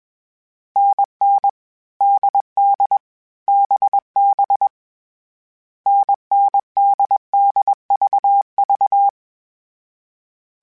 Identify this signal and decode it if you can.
Morse code, heard as a keyed tone, about 21 wpm, 800 Hz. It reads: NN DD BB NN DD VV
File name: Variable Speed / Variable Speed-mp3s / GZ GZ7 sp @20